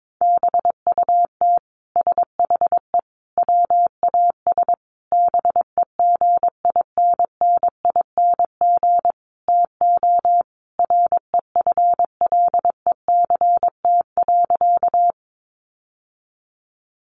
Morse code, heard as a keyed tone, about 22 wpm, 700 Hz. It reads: BUT SHE WAS BEGINNING TO REFLECT.
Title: Sentences from Top 1500 Words 22wpm